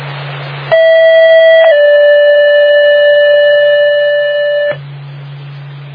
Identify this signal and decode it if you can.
Two-tone page, heard as a one-second tone, then a three-second tone, about 650 then 580 Hz, with noise. It is heard from 0.7 to 4.7 s.